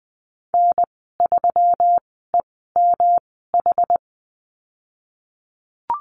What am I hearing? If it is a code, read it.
Morse code, heard as a keyed tone, about 20 wpm, spaced as for 15 wpm, 700 Hz. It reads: N3EMH
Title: 122 Lesson 20g - Number 3 - Call Signs 20wpm